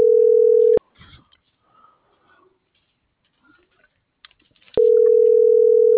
Ringback tone (North America), 440 + 480 Hz.